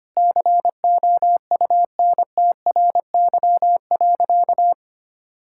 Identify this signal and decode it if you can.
Morse code, heard as a keyed tone, about 25 wpm, 700 Hz. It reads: COUNTRY.
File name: practice / 25wpm / Sentences with 3-Word Phrases - 25wpm